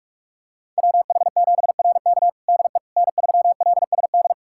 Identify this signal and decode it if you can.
Morse code, heard as a keyed tone, about 46 wpm, 700 Hz. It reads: WH7RK DE N3LSD
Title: USA Callsign DE Callsign - 5 Characters - 45wpm